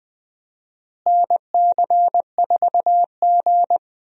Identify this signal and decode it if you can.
Morse code, heard as a keyed tone, about 20 wpm, 700 Hz. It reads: NC4G
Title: USA Call Signs - 4 Characters - Repeated 2x - 20wpm